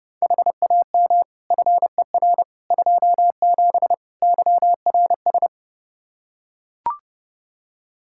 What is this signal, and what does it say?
Morse code, heard as a keyed tone, about 30 wpm, 700 Hz. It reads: HAM FER 27 YRS
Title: Practice QSO - Exchange 3 - More Info - 30wpm